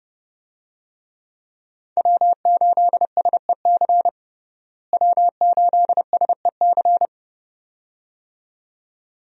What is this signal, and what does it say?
Morse code, heard as a keyed tone, about 30 wpm, 700 Hz. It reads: W8SEC W8SEC
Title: USA Call Signs - 5 Characters - Repeated 2x - 30wpm